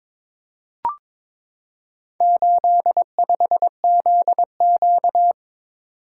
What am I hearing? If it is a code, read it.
Morse code, heard as a keyed tone, about 22 wpm, 700 Hz. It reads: 85ZQ
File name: practice / 22wpm / Four Letter-Number Combinations - 22wpm